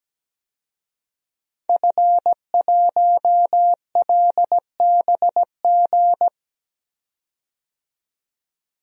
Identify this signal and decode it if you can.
Morse code, heard as a keyed tone, about 17 wpm, 700 Hz. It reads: F1LBG
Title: DXCC-100 Call Signs - 5 Characters - 17wpm